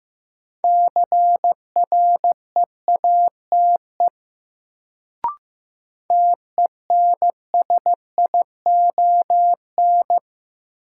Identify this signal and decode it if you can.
Morse code, heard as a keyed tone, about 16 wpm, 700 Hz. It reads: CREATE TENSION
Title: Sentences from Top 1600 Words - New Word Drill Alt - 15wpm